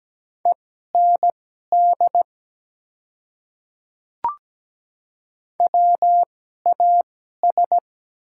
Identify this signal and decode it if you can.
Morse code, heard as a keyed tone, about 17 wpm, spaced as for 12 wpm, 700 Hz. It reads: END WAS
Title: 2x Character Spacing - Top 300 Words - 17wpm